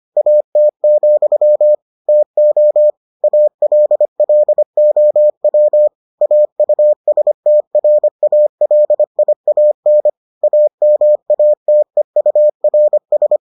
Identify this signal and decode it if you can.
Morse code, heard as a keyed tone, about 25 wpm, 600 Hz. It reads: AT, TO ALLOW AUSTRALIAN AMATEURS